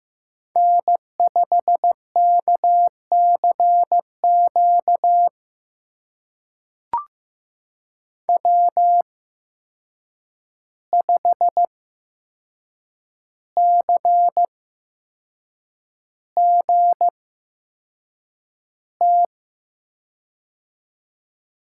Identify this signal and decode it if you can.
Morse code, heard as a keyed tone, about 15 wpm, spaced as for 6 wpm, 700 Hz. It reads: N5KCQ W5CGT
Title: USA Call Signs - 5 Characters - 8x Character Spacing - 15wpm